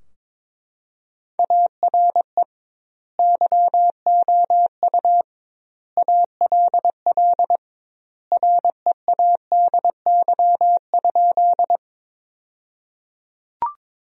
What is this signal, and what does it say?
Morse code, heard as a keyed tone, about 22 wpm, 700 Hz. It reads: ARE YOU ALL READY?
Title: 2x Word-Spacing - Sentences from Top 400 Words - 22wpm